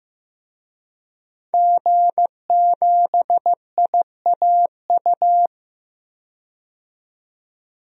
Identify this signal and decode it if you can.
Morse code, heard as a keyed tone, about 15 wpm, 700 Hz. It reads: G7IAU
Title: DXCC-100 Call Signs - 5 Characters - 15wpm